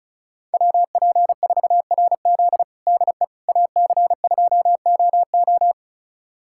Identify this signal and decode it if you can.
Morse code, heard as a keyed tone, about 36 wpm, 700 Hz. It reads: WP4RZ DE AC2OO
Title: USA Callsign DE Callsign - 5 Characters - 35wpm